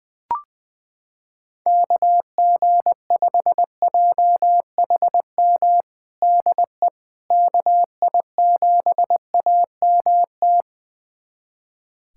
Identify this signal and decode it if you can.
Morse code, heard as a keyed tone, about 21 wpm, 700 Hz. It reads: KG5JHM DE KI7AMT
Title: USA Callsign DE Callsign - 6 Characters - 20wpm